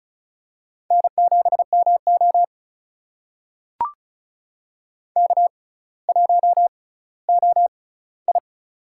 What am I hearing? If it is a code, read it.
Morse code, heard as a keyed tone, about 36 wpm, spaced as for 18 wpm, 700 Hz. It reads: N7MO K1OI
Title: USA Call Signs - 4 Characters - 6x Character Spacing - 35wpm